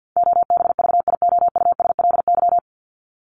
Morse code, heard as a keyed tone, about 50 wpm, 700 Hz. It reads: OBVIOUSLY
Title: LY Suffix - Top 100 Words with Base Words 50wpm